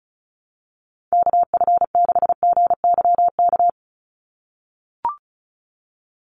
Morse code, heard as a keyed tone, about 34 wpm, 700 Hz. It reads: KF6GYK
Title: USA Call Signs 35wpm